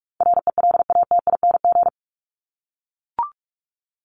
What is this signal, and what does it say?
Morse code, heard as a keyed tone, about 44 wpm, 700 Hz. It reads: RELATING
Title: ING Suffix - Top 100 Words with Base Words 45wpm